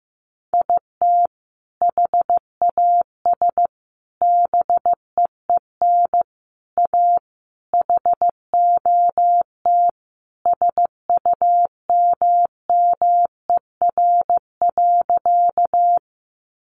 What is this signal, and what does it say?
Morse code, heard as a keyed tone, about 15 wpm, 700 Hz. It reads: IT HAS BEEN A HOT SUMMER.
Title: Sentences from Top 1100 Words 15wpm